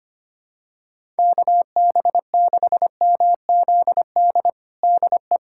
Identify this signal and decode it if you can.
Morse code, heard as a keyed tone, about 25 wpm, 700 Hz. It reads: KB6MZD DE
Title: USA Callsign DE Callsign - 6 Characters - 25wpm